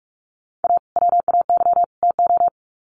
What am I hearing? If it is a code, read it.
Morse code, heard as a keyed tone, about 44 wpm, 700 Hz. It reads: A WAY TO
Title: Sets of 3 Words 45wpm